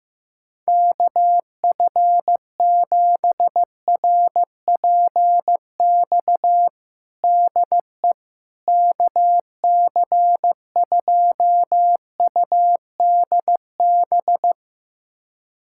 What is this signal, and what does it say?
Morse code, heard as a keyed tone, about 15 wpm, 700 Hz. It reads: KF7RPX DE KC2UDB
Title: USA Callsign DE Callsign - 6 Characters - 15wpm